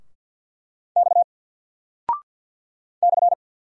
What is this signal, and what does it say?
Morse code, heard as a keyed tone, about 48 wpm, 700 Hz. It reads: X /
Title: Single Character (Letter:Number:Punctuation:Prosign) 50wpm